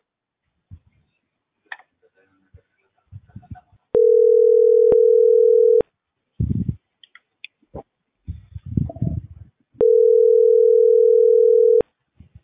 Ringback tone (North America), 440 + 480 Hz.